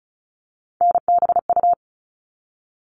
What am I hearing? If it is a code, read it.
Morse code, heard as a keyed tone, about 34 wpm, 700 Hz. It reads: NBU
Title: All Cut Numbers - 3-Digit 35wpm